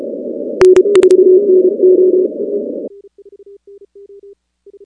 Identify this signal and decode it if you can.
Morse code, heard as a keyed tone, about 31 wpm, 370 Hz. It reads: N4NO
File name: contests / 2019-cw-wpx-kt5j-k5pi- -2330 0-16